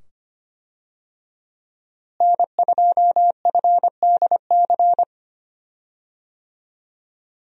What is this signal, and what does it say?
Morse code, heard as a keyed tone, about 25 wpm, 700 Hz. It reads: N2FDC
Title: USA Call Signs - 5 Characters - Repeated 2x - 25wpm